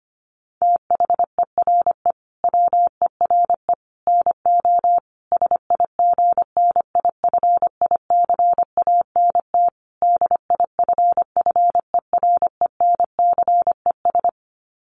Morse code, heard as a keyed tone, about 25 wpm, 700 Hz. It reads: THERE WERE NO SIGNIFICANT DIFFERENCES
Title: Sets of 5 Words Encore2 25wpm